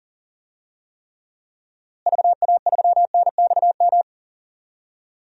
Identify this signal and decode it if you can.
Morse code, heard as a keyed tone, about 40 wpm, 700 Hz. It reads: VA3NXM
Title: DXCC-100 Call Signs - 6 Characters - Repeated 2x - 40wpm